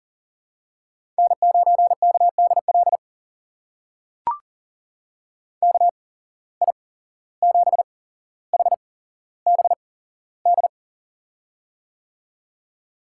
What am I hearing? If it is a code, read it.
Morse code, heard as a keyed tone, about 39 wpm, spaced as for 16 wpm, 700 Hz. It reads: N9KDL KI7HBD
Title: USA Call Signs - 8x Character Spacing - 40wpm